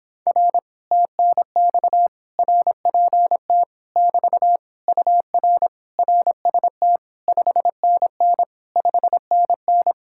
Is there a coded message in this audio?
Yes: Morse code from 0.3 to 9.9 s, heard as a keyed tone, about 26 wpm, 700 Hz. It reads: R TNX RPT <BT> UR RST 5NN 5NN